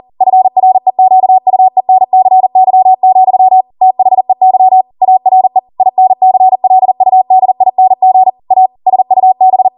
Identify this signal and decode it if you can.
Morse code, heard as a keyed tone, about 40 wpm, 750 Hz. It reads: FREQUENCY, THEY ARE INCLUDING A SUB
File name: Morse / 40WPM